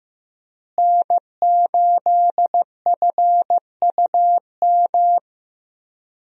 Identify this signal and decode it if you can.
Morse code, heard as a keyed tone, about 16 wpm, 700 Hz. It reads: N8FUM
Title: USA Call Signs - 5 Characters - Repeated 2x - 15wpm